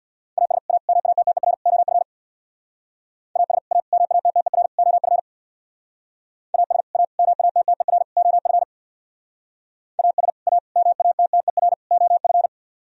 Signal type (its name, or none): Morse code